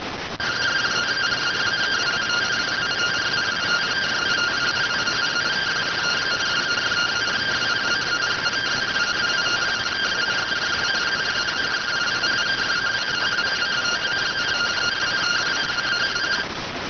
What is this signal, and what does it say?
Two-tone teleprinter signal, about 170 Hz shift.